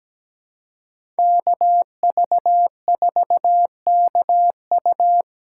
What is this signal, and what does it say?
Morse code, heard as a keyed tone, about 17 wpm, 700 Hz. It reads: KV4KU
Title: USA Callsign DE Callsign 17wpm